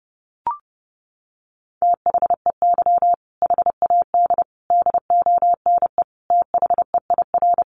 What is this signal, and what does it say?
Morse code, heard as a keyed tone, about 30 wpm, 700 Hz. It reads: THEY HAD DONE THEIR
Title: Sentences from Top 1500 Words 30wpm